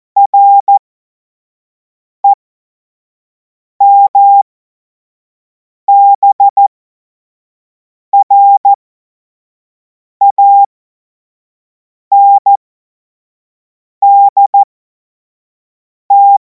Morse code, heard as a keyed tone, about 14 wpm, spaced as for 5 wpm, 800 Hz. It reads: REMBRANDT
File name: Morsekurs / Witze / Witz 31. März 2026 25 F70